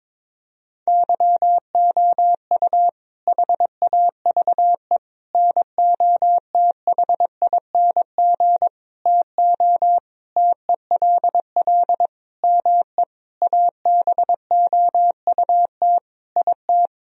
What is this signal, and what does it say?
Morse code, heard as a keyed tone, about 22 wpm, 700 Hz. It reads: YOU HAVE NOTHING TO TELL ME ABOUT IT
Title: Sentences from Top 1000 Words - Review Part 4 - 22wpm